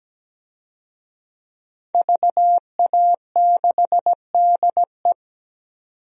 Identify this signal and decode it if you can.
Morse code, heard as a keyed tone, about 17 wpm, 700 Hz. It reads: VA6DE
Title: DXCC-100 Call Signs - 5 Characters - 17wpm